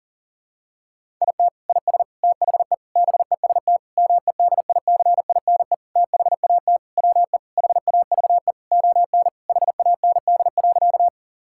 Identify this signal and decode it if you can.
Morse code, heard as a keyed tone, about 40 wpm, 700 Hz. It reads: IT IS THE BEST MEDICINE THAT WE HAVE ON HAND.
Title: Sentences from Top 1000 Words - Review Part 6 - 40wpm